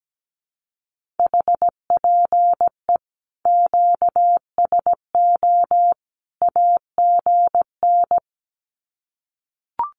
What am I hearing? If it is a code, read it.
Morse code, heard as a keyed tone, about 17 wpm, 700 Hz. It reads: HPE QSO AGN
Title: Common QSO Phrases 17wpm